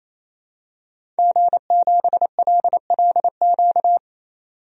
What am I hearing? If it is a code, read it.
Morse code, heard as a keyed tone, about 28 wpm, 700 Hz. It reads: G7LLQ
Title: DXCC-100 Call Signs - 5 Characters - 28wpm